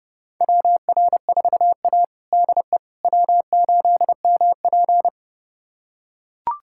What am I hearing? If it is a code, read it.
Morse code, heard as a keyed tone, about 32 wpm, 700 Hz. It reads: WR4A DE W8MP
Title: USA Callsign DE Callsign - 4 Characters - 30wpm